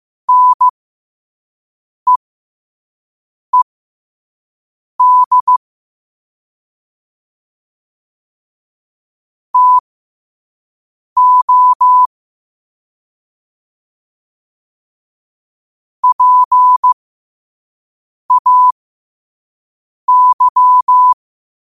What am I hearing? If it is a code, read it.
Morse code, heard as a keyed tone, about 15 wpm, spaced as for 5 wpm, 1000 Hz. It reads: NEED TO PAY
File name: qotdfiles / 5wpm